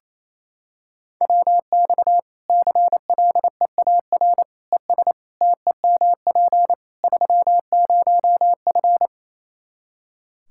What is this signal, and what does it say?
Morse code, heard as a keyed tone, about 28 wpm, 700 Hz. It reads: WX CLEAR ES TEMP 30F
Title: Practice QSO - Exchange 2 - Info - 28wpm